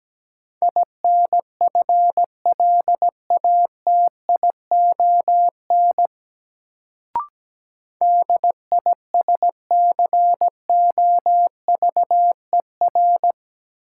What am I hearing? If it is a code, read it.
Morse code, heard as a keyed tone, about 17 wpm, 700 Hz. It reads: INFLATION DISCOVER
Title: Sentences from Top 1100 Words - New Word Drill Alt - 17wpm